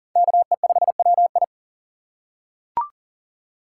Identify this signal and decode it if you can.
Morse code, heard as a keyed tone, about 40 wpm, 700 Hz. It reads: KE5WI
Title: USA Call Signs - 5 Characters - 2x Character Spacing - 40wpm